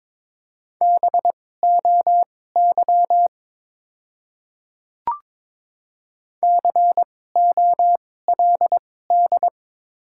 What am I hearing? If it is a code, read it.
Morse code, heard as a keyed tone, about 22 wpm, spaced as for 16 wpm, 690 Hz. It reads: BOY COLD
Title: 2x Character Spacing - Top 500 Words - 22wpm